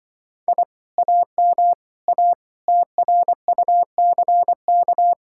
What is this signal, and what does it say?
Morse code, heard as a keyed tone, about 24 wpm, 700 Hz. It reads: I AM A TRUCK
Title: Speed-Racing Common QSO Phrases - 20wpm